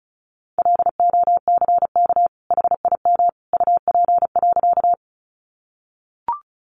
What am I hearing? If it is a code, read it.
Morse code, heard as a keyed tone, about 34 wpm, 700 Hz. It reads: LOCK HIM UP.
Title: Sentences from Top 1300 Words 35wpm